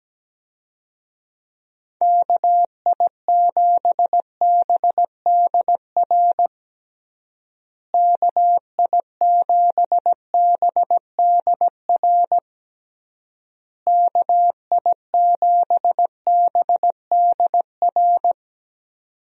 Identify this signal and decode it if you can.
Morse code, heard as a keyed tone, about 17 wpm, 700 Hz. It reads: KI7BDR KI7BDR KI7BDR
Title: USA Call Signs - 6 Characters - Repeated 3x - 17wpm